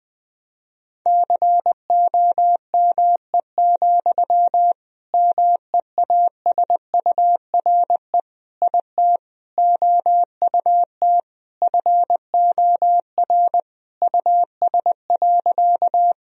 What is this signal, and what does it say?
Morse code, heard as a keyed tone, about 20 wpm, 700 Hz. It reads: COME, MEASURE IT OUT FOR US.
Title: Sentences from Top 1000 Words - Review Part 2 - 20wpm